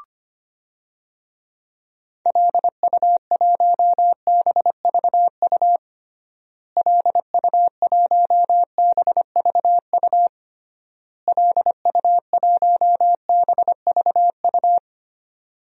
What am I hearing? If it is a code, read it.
Morse code, heard as a keyed tone, about 25 wpm, 700 Hz. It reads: LU1BVU LU1BVU LU1BVU
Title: DXCC-100 Call Signs - 6 Characters - Repeated 3x - 25wpm